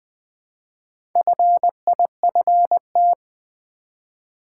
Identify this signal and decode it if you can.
Morse code, heard as a keyed tone, about 20 wpm, 700 Hz. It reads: FIFT
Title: Build A Word - 1 Letter At A Time - 5-Letter Words - 20wpm